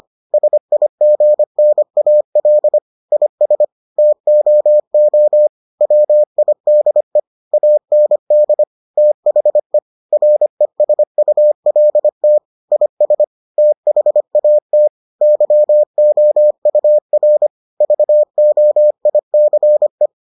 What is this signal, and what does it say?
Morse code, heard as a keyed tone, about 25 wpm, 600 Hz. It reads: SIGNAL IS TOO WIDE AND THE RESULT IS THAT YOUR VOICE